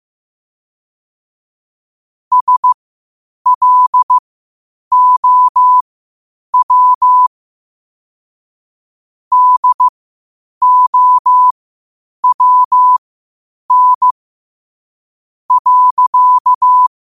Morse code, heard as a keyed tone, about 15 wpm, spaced as for 8 wpm, 1000 Hz. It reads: SLOW DOWN.
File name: qotdfiles / 8wpm